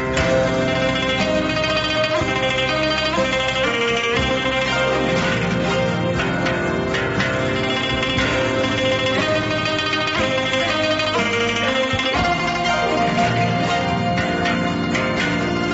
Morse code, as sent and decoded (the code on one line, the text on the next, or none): none